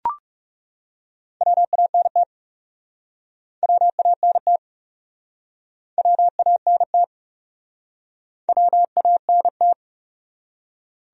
Morse code, heard as a keyed tone, about 40 wpm, 700 Hz. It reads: WANT WANT WANT WANT